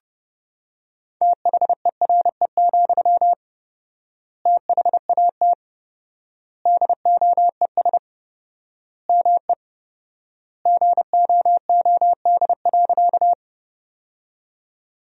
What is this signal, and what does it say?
Morse code, heard as a keyed tone, about 30 wpm, 700 Hz. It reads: THERE, THAT DOES ME GOOD.
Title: 4x Word-Spacing - Sentences from Top 200 Words - 30wpm